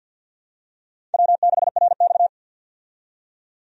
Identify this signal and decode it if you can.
Morse code, heard as a keyed tone, about 53 wpm, 700 Hz. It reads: W6RX
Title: USA Call Signs - 4 Characters - 50wpm